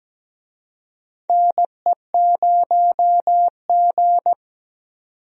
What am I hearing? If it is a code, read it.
Morse code, heard as a keyed tone, about 17 wpm, 700 Hz. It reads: NE0G
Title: USA Call Signs - 4 Characters - 17wpm